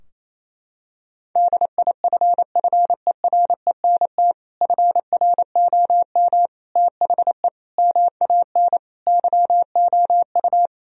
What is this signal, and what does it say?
Morse code, heard as a keyed tone, about 28 wpm, 700 Hz. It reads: DIFFERENT FROM THE MAN YOU
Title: Sentences from Top 1000 Words - Review Part 3 - 28wpm